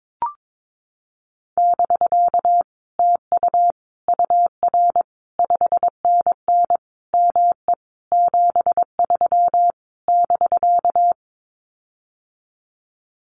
Morse code, heard as a keyed tone, about 22 wpm, 700 Hz. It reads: <BK> TU UR 5NN ME 73 <BK>
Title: Common QSO Phrases Encore 22wpm